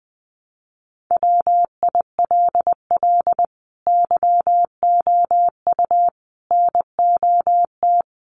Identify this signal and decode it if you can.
Morse code, heard as a keyed tone, about 20 wpm, 700 Hz. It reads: WILL YOU NOT
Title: Sentences from Top 1300 Words 20wpm